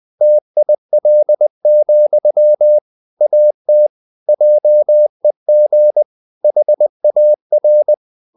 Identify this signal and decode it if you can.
Morse code, heard as a keyed tone, about 20 wpm, 600 Hz. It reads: TIL, AT JEG HAR
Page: Ferie 20 wpm